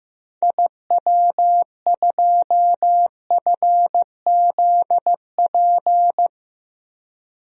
Morse code, heard as a keyed tone, about 15 wpm, 700 Hz. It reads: IW2FZP
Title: DXCC-100 Call Signs - 6 Characters - 15wpm